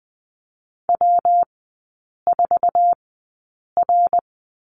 Morse code, heard as a keyed tone, about 20 wpm, spaced as for 8 wpm, 690 Hz. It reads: W4R
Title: 159 Lesson 26e - Letter g - Three Characters 20wpm